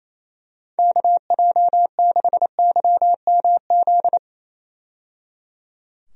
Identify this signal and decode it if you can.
Morse code, heard as a keyed tone, about 27 wpm, 700 Hz. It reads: KJ6YMZ
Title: USA Call Signs - 6 Characters - 28wpm